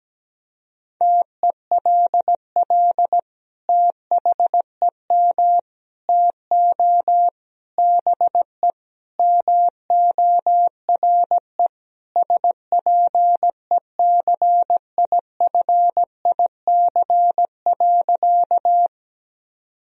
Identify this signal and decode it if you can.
Morse code, heard as a keyed tone, about 17 wpm, 700 Hz. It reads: TELL THEM TO BE MORE SPECIFIC.
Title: Sentences from Top 1000 Words - Review Part 3 - 17wpm